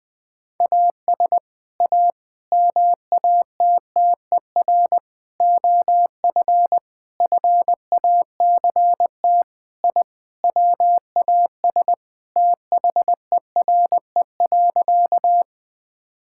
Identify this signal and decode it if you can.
Morse code, heard as a keyed tone, about 20 wpm, 700 Hz. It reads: AS A MATTER OF FACT I WAS THERE.
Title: Sentences with 4-Word Phrases - Spoken Beginning - 20wpm